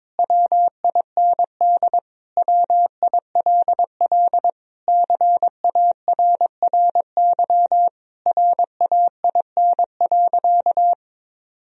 Morse code, heard as a keyed tone, about 22 wpm, 700 Hz. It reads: WIND WILL CARRY RAIN.
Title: Sentences from Top 1000 Words - Review Part 3 - 22wpm